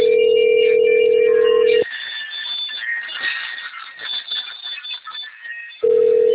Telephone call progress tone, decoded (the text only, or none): ringback tone